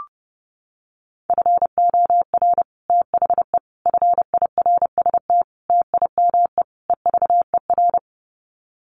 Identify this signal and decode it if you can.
Morse code, heard as a keyed tone, about 28 wpm, 700 Hz. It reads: FOR THE FIRST TIME EVER
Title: Sets of 5 Words Encore2 30wpm